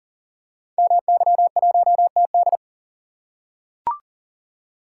Morse code, heard as a keyed tone, about 40 wpm, 700 Hz. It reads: MY1TD
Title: Code Groups - 5 Letter:Numbers - 40wpm